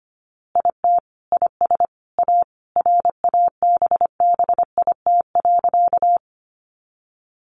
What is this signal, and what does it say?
Morse code, heard as a keyed tone, about 25 wpm, 700 Hz. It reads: IT IS A RABBIT.
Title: Easy Phrases 25wpm